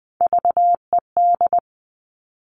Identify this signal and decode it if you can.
Morse code, heard as a keyed tone, about 20 wpm, 700 Hz. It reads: VED
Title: All Cut Numbers - 3-Digit 20wpm